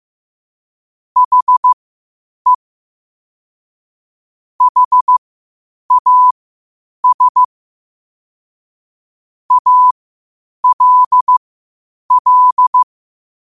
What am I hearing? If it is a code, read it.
Morse code, heard as a keyed tone, about 15 wpm, spaced as for 8 wpm, 1000 Hz. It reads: HE HAS ALL
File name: qotdfiles / 8wpm